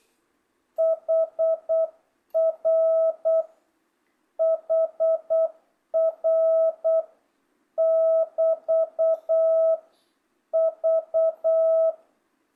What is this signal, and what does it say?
Morse code, heard as a keyed tone, about 8 wpm, 660 Hz. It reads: HR HR <BT> V
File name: jh8cbh / 2025cw